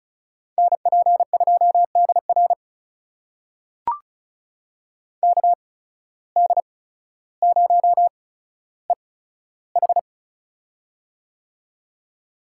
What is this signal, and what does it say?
Morse code, heard as a keyed tone, about 36 wpm, spaced as for 15 wpm, 700 Hz. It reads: NP2DR KD0EH
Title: USA Call Signs - 5 Characters - 8x Character Spacing - 35wpm